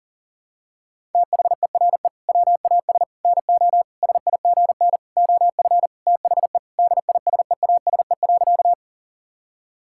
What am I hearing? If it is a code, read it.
Morse code, heard as a keyed tone, about 37 wpm, 700 Hz. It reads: THERE WAS NO SIGN OF THE DISEASE.
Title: Sentences from Top 1000 Words - Review Part 6 - 40wpm